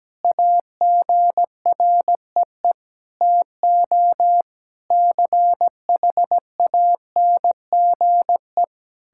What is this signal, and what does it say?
Morse code, heard as a keyed tone, about 17 wpm, 700 Hz. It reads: AGREE TO CHANGE
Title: Sentences from Top 1000 Words - Review Part 6 - 17wpm